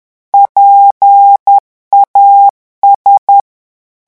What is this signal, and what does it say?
Morse code, heard as a keyed tone, about 10 wpm, 800 Hz. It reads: PAS
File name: morse mp3 / revision9 citations humoristiques vitesse 10 mots minute